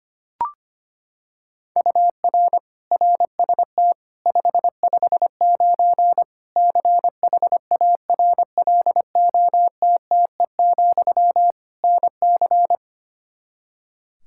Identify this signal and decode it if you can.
Morse code, heard as a keyed tone, about 25 wpm, 700 Hz. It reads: UR RST 559 CHARLOTTE, NC
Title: Practice QSO - Exchange 1 - Beginning - 25wpm